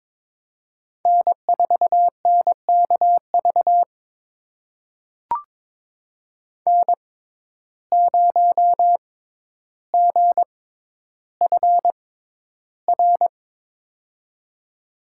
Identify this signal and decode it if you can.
Morse code, heard as a keyed tone, about 22 wpm, spaced as for 11 wpm, 700 Hz. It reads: N4NKV N0GFR
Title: USA Call Signs - 5 Characters - 6x Character Spacing - 22wpm